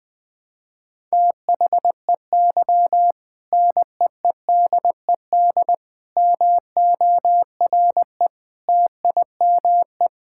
Morse code, heard as a keyed tone, about 20 wpm, 700 Hz. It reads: THEY NEEDED MORE TIME
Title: Sentences with 2-Word Phrases - Spoken Beginning - 20wpm